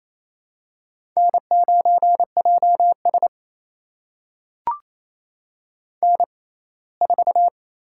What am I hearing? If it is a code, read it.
Morse code, heard as a keyed tone, about 28 wpm, spaced as for 19 wpm, 700 Hz. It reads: N9JS N4
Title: USA Call Signs - 6x Character Spacing - 28wpm